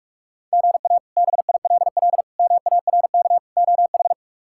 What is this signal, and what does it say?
Morse code, heard as a keyed tone, about 42 wpm, 700 Hz. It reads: GA BILL MARK OH
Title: K1USN SST QSO Practice - 45wpm